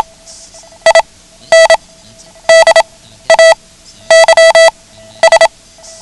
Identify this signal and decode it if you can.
Morse code, heard as a keyed tone, about 27 wpm, spaced as for 12 wpm, 660 Hz. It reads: IN DAYS
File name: ham radio / CodeQuick / Code Quick 09